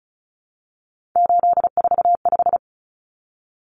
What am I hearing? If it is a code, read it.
Morse code, heard as a keyed tone, about 34 wpm, 700 Hz. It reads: 845
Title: Common Cut Numbers - 3-Digit 35wpm